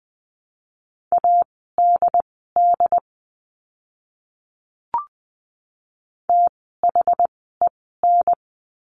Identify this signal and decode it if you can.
Morse code, heard as a keyed tone, about 20 wpm, spaced as for 15 wpm, 700 Hz. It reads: ADD THEN
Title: 101 Lesson 17f - Letter u - Words 20wpm